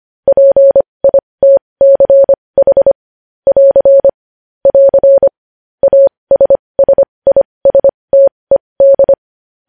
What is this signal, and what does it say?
Morse code, heard as a keyed tone, about 25 wpm, spaced as for 20 wpm, 550 Hz. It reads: PITCH <AR> <AR> ASSISTED
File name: podcast / News headlines-1x-25x20